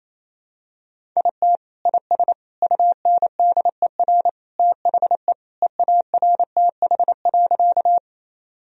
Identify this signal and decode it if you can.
Morse code, heard as a keyed tone, about 28 wpm, 700 Hz. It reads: IT IS UNDER THE EARTH.
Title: Sentences from Top 1000 Words - Review Part 1 - 28wpm